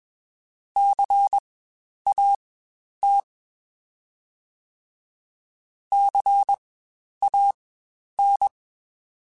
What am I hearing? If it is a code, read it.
Morse code, heard as a keyed tone, about 21 wpm, spaced as for 10 wpm, 800 Hz. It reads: CAT CAN
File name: Spaced Alphas / Spaced Alphas mp3s / Cs sp @20